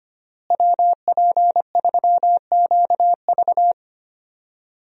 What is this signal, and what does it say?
Morse code, heard as a keyed tone, about 25 wpm, 700 Hz. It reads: WP3QV
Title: USA Call Signs - 5 Characters - Repeated 3x - 25wpm